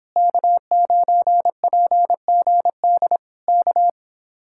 Morse code, heard as a keyed tone, about 26 wpm, 700 Hz. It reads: K9PGD K
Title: Speed-Racing Common QSO Phrases - 22wpm